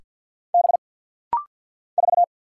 Morse code, heard as a keyed tone, about 49 wpm, 700 Hz. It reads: B 4
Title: Single Letter-Number - Rapid-Fire 50wpm